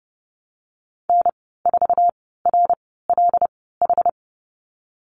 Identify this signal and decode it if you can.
Morse code, heard as a keyed tone, about 30 wpm, spaced as for 17 wpm, 700 Hz. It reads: N4RLH
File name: Lesson 19g - Letter w - Call Signs 30wpm